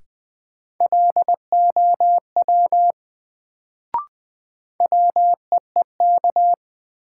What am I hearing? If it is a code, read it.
Morse code, heard as a keyed tone, about 20 wpm, 700 Hz. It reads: LOW WEEK
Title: Sentences from Top 500 Words - New Word Drill - 20wpm